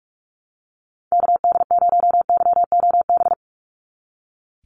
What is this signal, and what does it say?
Morse code, heard as a keyed tone, about 46 wpm, 700 Hz. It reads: KD0YOB
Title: NA SOTA Activator Callsigns 45wpm